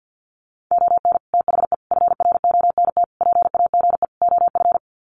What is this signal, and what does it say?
Morse code, heard as a keyed tone, about 50 wpm, 700 Hz. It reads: ON THE FRONT PAGE OF
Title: Sets of 5 Words Encore 50wpm